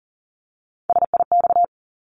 Morse code, heard as a keyed tone, about 40 wpm, 700 Hz. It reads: SIX
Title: Top 500 Words 40wpm